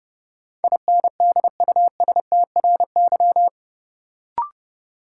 Morse code, heard as a keyed tone, about 32 wpm, 700 Hz. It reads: INDUSTRY
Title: Sentences from Top 1000 Words - Review Part 4 - New Word Drill - 30wpm